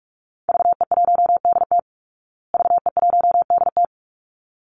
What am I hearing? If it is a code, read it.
Morse code, heard as a keyed tone, about 44 wpm, 700 Hz. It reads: VE1DT VE1DT
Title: Call Signs Repeated 3x - 45wpm